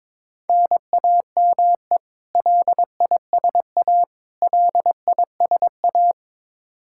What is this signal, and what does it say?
Morse code, heard as a keyed tone, about 22 wpm, 700 Hz. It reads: NAME LISA LISA
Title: Practice QSO - Exchange 1 - Beginning - 22wpm